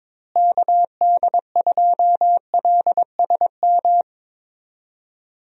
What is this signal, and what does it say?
Morse code, heard as a keyed tone, about 22 wpm, 700 Hz. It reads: KD2LSM
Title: USA Call Signs - 6 Characters - Repeated 3x - 22wpm